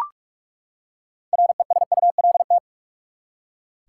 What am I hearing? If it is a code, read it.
Morse code, heard as a keyed tone, about 43 wpm, 700 Hz. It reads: RESULT
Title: Six Letter Words - 45wpm